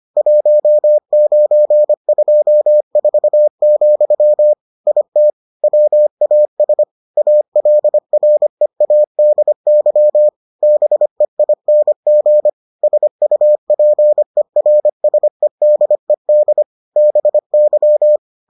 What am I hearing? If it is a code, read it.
Morse code, heard as a keyed tone, about 25 wpm, 600 Hz. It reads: 1924, IT WAS ALREADY BEING SUPERSEDED BY